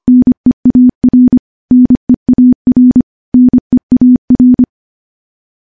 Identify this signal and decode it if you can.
Morse code, heard as a keyed tone, about 25 wpm, 250 Hz. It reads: NEAR NEAR NEAR
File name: morse-training-25WPM